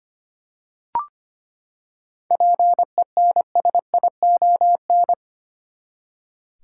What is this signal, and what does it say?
Morse code, heard as a keyed tone, about 25 wpm, 700 Hz. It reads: PENSION
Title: Seven Letter Words - 25wpm